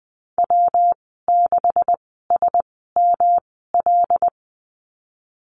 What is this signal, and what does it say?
Morse code, heard as a keyed tone, about 20 wpm, spaced as for 15 wpm, 700 Hz. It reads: W6SML
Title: 179 Lesson 29g - Slash - Call Signs 20wpm